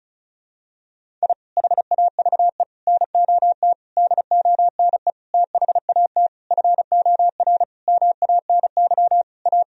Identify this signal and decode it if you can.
Morse code, heard as a keyed tone, about 34 wpm, 700 Hz. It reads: I HAVE NOT DONE THAT FOR MANY A
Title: Sentences from Top 1000 Words - Review Part 1 - 35wpm